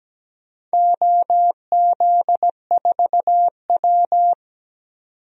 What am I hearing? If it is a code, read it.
Morse code, heard as a keyed tone, about 17 wpm, 700 Hz. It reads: OZ4W
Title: DXCC-100 Call Signs - 4 Characters - 17wpm